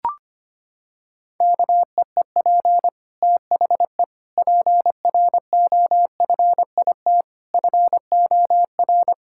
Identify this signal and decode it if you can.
Morse code, heard as a keyed tone, about 25 wpm, 700 Hz. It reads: KEEP THE PROFIT FOR
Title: Sentences from Top 1000 Words - Review Part 5 - 25wpm